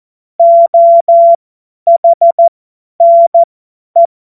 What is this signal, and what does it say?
Morse code, heard as a keyed tone, about 14 wpm, spaced as for 10 wpm, 680 Hz. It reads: OHNE
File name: Morsekurs / Witze / Witz 04. April 2023 50 F70